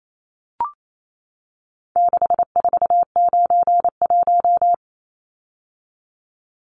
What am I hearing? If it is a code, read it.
Morse code, heard as a keyed tone, about 28 wpm, 700 Hz. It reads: 6491
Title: Numbers - 4-Digit 28wpm